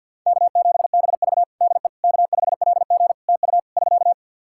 Morse code, heard as a keyed tone, about 49 wpm, 700 Hz. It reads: K7BV DE K5LG TU <SK>
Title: K1USN SST QSO Practice 50wpm